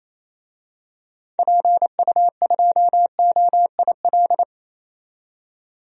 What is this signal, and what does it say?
Morse code, heard as a keyed tone, about 27 wpm, 700 Hz. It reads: PU2OIL
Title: DXCC-100 Call Signs - 6 Characters - 28wpm